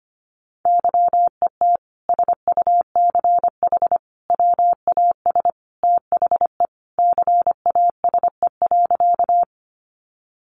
Morse code, heard as a keyed tone, about 25 wpm, 700 Hz. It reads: YET SUCH WAS THE CASE.
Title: Sentences from Top 700 Words 25wpm